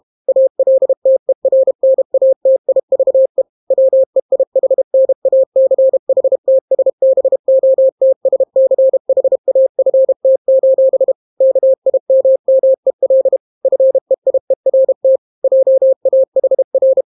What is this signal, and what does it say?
Morse code, heard as a keyed tone, about 32 wpm, 530 Hz. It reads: ALTERNATIVE WEIHNACHTSBOTSCHAFT: KIMMEL FEIERT JAHR